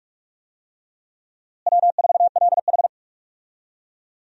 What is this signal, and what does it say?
Morse code, heard as a keyed tone, about 46 wpm, 700 Hz. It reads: W4LH
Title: USA Call Signs - 4 Characters - Repeated 2x - 45wpm